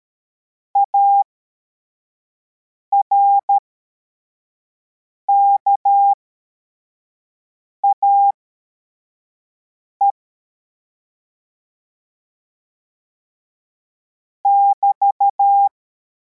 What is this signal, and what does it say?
Morse code, heard as a keyed tone, about 13 wpm, spaced as for 4 wpm, 800 Hz. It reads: ARKAE <BT>